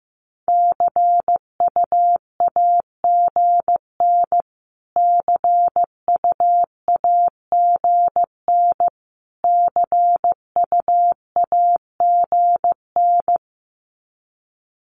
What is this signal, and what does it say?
Morse code, heard as a keyed tone, about 15 wpm, 700 Hz. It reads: CUAGN CUAGN CUAGN
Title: Abbreviations Repeated 3x - 15wpm